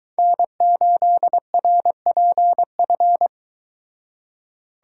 Morse code, heard as a keyed tone, about 23 wpm, 700 Hz. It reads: N8RPF